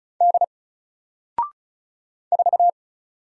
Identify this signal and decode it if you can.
Morse code, heard as a keyed tone, about 35 wpm, 700 Hz. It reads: D 4
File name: Single-Letter-Number-35wpm